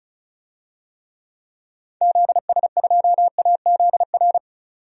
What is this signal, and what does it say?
Morse code, heard as a keyed tone, about 34 wpm, 700 Hz. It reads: ZS2AZR
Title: DXCC-100 Call Signs - 6 Characters - 35wpm